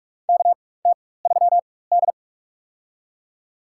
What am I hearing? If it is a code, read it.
Morse code, heard as a keyed tone, about 42 wpm, spaced as for 20 wpm, 700 Hz. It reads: KT3D
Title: USA Call Signs - 4 Characters - 4x Character Spacing - 45wpm